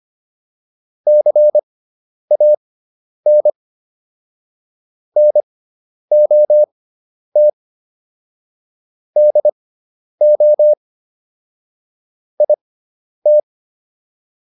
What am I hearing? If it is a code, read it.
Morse code, heard as a keyed tone, about 25 wpm, spaced as for 10 wpm, 600 Hz. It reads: CAN NOT DO IT
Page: intermediate-practice-files